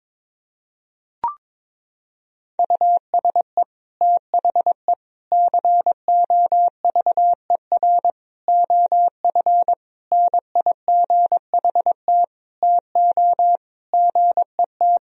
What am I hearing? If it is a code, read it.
Morse code, heard as a keyed tone, about 22 wpm, 700 Hz. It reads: USE THE COVER OF NIGHT TO GET
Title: Sentences from Top 1000 Words - Review Part 1 - 22wpm